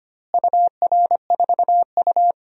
Morse code, heard as a keyed tone, about 25 wpm, 700 Hz. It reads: UR4U
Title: DXCC-100 Call Signs - 4 Characters - Repeated 2x - 25wpm